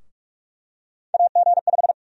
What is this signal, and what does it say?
Morse code, heard as a keyed tone, about 45 wpm, 700 Hz. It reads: AG5
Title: USA Call Sign Prefixes - Repeated 2x - 45wpm